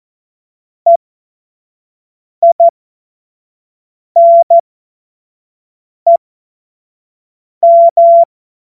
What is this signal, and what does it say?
Morse code, heard as a keyed tone, about 14 wpm, spaced as for 5 wpm, 680 Hz. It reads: EINEM
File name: Morsekurs / Witze / Witz 04. Januar 2022, 25 F70